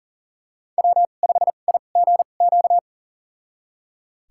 Morse code, heard as a keyed tone, about 40 wpm, spaced as for 29 wpm, 700 Hz. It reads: W5IGQ
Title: USA Call Signs - 5 Characters - 2x Character Spacing - 40wpm